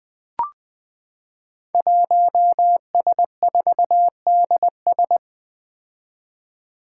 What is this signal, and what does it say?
Morse code, heard as a keyed tone, about 20 wpm, 700 Hz. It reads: 1S4DS
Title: Five Challenging Letter-Number Combinations 20wpm